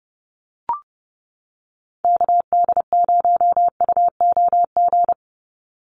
Morse code, heard as a keyed tone, about 30 wpm, 700 Hz. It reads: KD0UOG
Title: USA Call Signs 30wpm